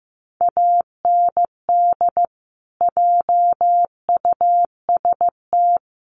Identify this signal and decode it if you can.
Morse code, heard as a keyed tone, about 15 wpm, 700 Hz. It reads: AND JUST
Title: Sentences from Top 1400 Words 15wpm